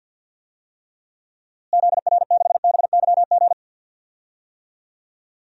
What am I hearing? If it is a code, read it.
Morse code, heard as a keyed tone, about 51 wpm, 700 Hz. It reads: ZR6BYG
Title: DXCC-100 Call Signs - 6 Characters - Repeated 2x - 50wpm